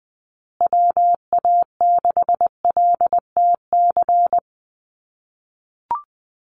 Morse code, heard as a keyed tone, about 20 wpm, 700 Hz. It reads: WA6LTC
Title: USA Call Signs Repeated 3x 20wpm